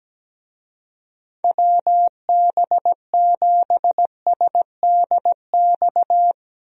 Morse code, heard as a keyed tone, about 17 wpm, 700 Hz. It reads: WB7SDX
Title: USA Call Signs - 6 Characters - Repeated 3x - 17wpm